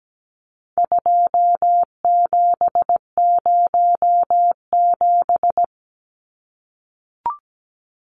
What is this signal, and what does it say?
Morse code, heard as a keyed tone, about 17 wpm, 700 Hz. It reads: 2707
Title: Numbers - 4-Digit 17wpm